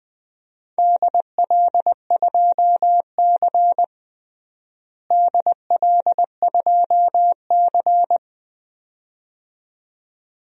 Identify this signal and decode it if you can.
Morse code, heard as a keyed tone, about 21 wpm, 700 Hz. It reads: DL2C DL2C
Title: DXCC-100 Call Signs - 4 Characters - Repeated 2x - 20wpm